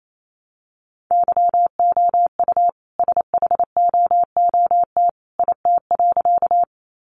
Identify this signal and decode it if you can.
Morse code, heard as a keyed tone, about 28 wpm, 700 Hz. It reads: YOU SHOOT IT.
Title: A Sound Of Thunder 28wpm